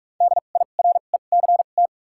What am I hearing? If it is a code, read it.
Morse code, heard as a keyed tone, about 46 wpm, spaced as for 33 wpm, 700 Hz. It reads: DIRECT
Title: 2x Character Spacing - Top 500 Words - 45wpm